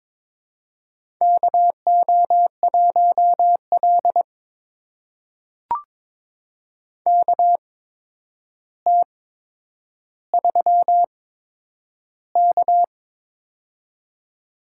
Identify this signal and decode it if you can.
Morse code, heard as a keyed tone, about 23 wpm, spaced as for 10 wpm, 700 Hz. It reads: KO1L KT3K
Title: USA Call Signs - 4 Characters - 8x Character Spacing - 22wpm